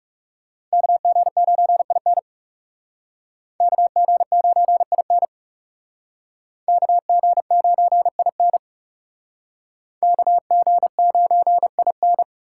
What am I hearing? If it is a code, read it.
Morse code, heard as a keyed tone, about 38 wpm, 700 Hz. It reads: KG9IN KG9IN KG9IN KG9IN